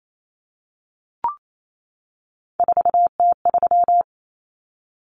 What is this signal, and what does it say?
Morse code, heard as a keyed tone, about 28 wpm, 700 Hz. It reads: 4T3
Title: Common Cut Numbers - 3-Digit 28wpm